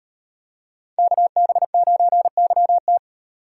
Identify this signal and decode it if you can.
Morse code, heard as a keyed tone, about 36 wpm, 700 Hz. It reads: KB9YT